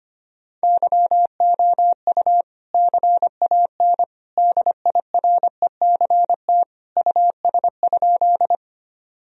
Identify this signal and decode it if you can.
Morse code, heard as a keyed tone, about 25 wpm, 700 Hz. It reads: YOU CAN DIRECT US?
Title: Sentences from Top 1000 Words - Review Part 2 - 25wpm